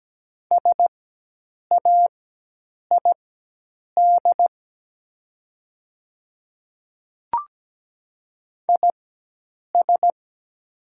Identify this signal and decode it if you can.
Morse code, heard as a keyed tone, about 17 wpm, spaced as for 8 wpm, 700 Hz. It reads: SAID IS
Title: 4x Character Spacing - Top 300 Words - 17wpm